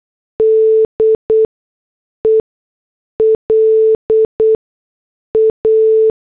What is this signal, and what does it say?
Morse code, heard as a keyed tone, about 8 wpm, spaced as for 6 wpm, 440 Hz. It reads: DELA